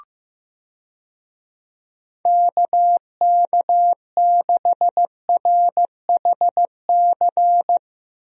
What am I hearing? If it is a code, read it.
Morse code, heard as a keyed tone, about 16 wpm, 700 Hz. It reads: KK6RHC
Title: USA Call Signs - 6 Characters - Repeated 3x - 15wpm